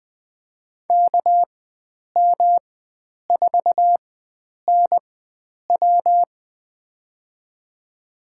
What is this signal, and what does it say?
Morse code, heard as a keyed tone, about 20 wpm, spaced as for 9 wpm, 700 Hz. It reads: KM4NW
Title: USA Call Signs - 5 Characters - 4x Character Spacing - 20wpm